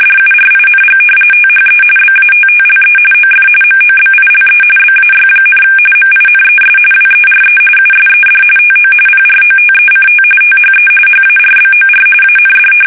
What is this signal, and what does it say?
RTTY, about 45.45 baud. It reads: DURING THE BATTLE OF MOBILE BAY, MINES WERE USED TO PROTECT THE HARBOR AND
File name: fsk850 / usnavy3